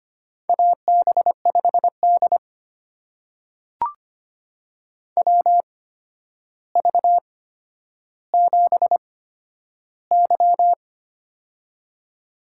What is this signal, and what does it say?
Morse code, heard as a keyed tone, about 25 wpm, spaced as for 11 wpm, 700 Hz. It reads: AB5D WV7Y
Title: USA Call Signs - 4 Characters - 8x Character Spacing - 25wpm